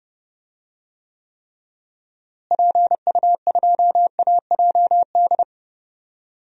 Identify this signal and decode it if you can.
Morse code, heard as a keyed tone, about 30 wpm, 700 Hz. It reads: PU2AJD
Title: DXCC-100 Call Signs - 6 Characters - Repeated 3x - 30wpm